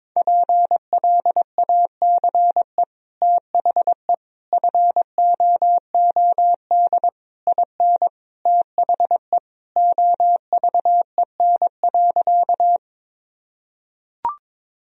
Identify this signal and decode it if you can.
Morse code, heard as a keyed tone, about 22 wpm, 700 Hz. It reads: PLACE THE FOOD IN THE OVEN.
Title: Sentences from Top 1000 Words - Review Part 4 - 22wpm